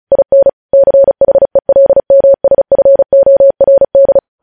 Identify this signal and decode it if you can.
Morse code, heard as a keyed tone, about 36 wpm, 550 Hz. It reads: IN CHELMSFORD
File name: podcast / News headlines-1x-35x35